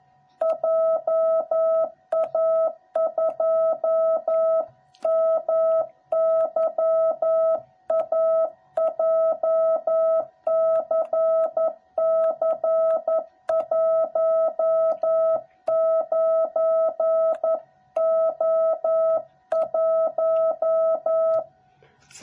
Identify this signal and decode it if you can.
Morse code, heard as a keyed tone, about 11 wpm, 660 Hz. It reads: JA2MYA JCC19O1
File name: jh8cbh / 2021cw / cw06b